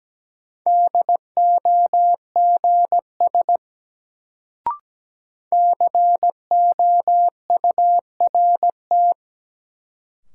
Morse code, heard as a keyed tone, about 17 wpm, 700 Hz. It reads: DOGS COURT
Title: Sentences from Top 1600 Words - New Word Drill - 17wpm